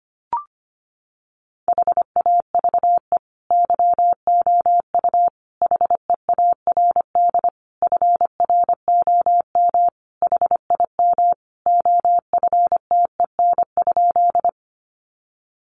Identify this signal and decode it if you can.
Morse code, heard as a keyed tone, about 25 wpm, 700 Hz. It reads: HAVE YOU HEARD FROM HIM OFTEN?
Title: Sentences from Top 400 Words 25wpm